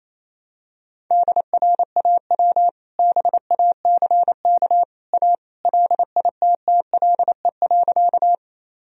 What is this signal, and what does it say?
Morse code, heard as a keyed tone, about 28 wpm, 700 Hz. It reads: DRAW BACK A LITTLE.
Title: Sentences from Top 1000 Words - Review Part 1 - 28wpm